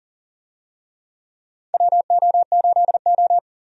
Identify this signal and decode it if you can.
Morse code, heard as a keyed tone, about 40 wpm, 700 Hz. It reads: WO8O
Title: USA Call Signs - 4 Characters - 40wpm